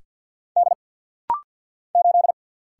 Morse code, heard as a keyed tone, about 50 wpm, 700 Hz. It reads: D 8
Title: Single Letter-Number Alt - Rapid-Fire 50wpm